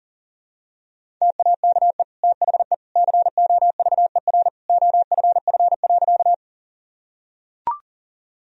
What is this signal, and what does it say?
Morse code, heard as a keyed tone, about 40 wpm, 700 Hz. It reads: TAKE THE COVER OFF.
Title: Sentences from Top 1000 Words - Review Part 1 - 40wpm